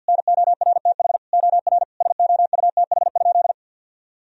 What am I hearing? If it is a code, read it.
Morse code, heard as a keyed tone, about 51 wpm, 700 Hz. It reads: NORTH OR SOUTH?
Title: Sentences from Top 1000 Words - Review Part 2 - 50wpm